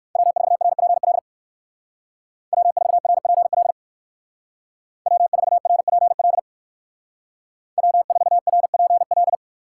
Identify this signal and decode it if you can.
Morse code, heard as a keyed tone, about 64 wpm, 700 Hz. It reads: W4RPL W4RPL W4RPL W4RPL